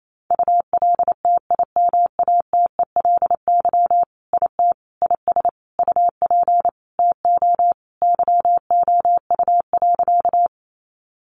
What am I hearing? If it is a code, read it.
Morse code, heard as a keyed tone, about 28 wpm, 700 Hz. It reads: ULTIMATELY IT IS UP TO YOU.
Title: Sentences from Top 1200 Words 28wpm